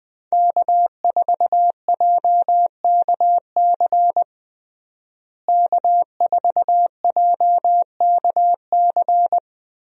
Morse code, heard as a keyed tone, about 20 wpm, 700 Hz. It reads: K4JKC K4JKC
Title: USA Call Signs - 5 Characters - Repeated 3x - 20wpm